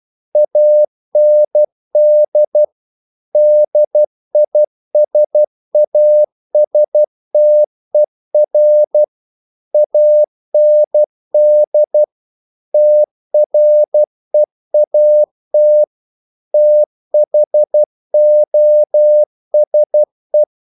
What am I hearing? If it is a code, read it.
Morse code, heard as a keyed tone, about 12 wpm, 600 Hz. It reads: AND DISASTER AND TREAT THOSE
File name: CW / Practice / If12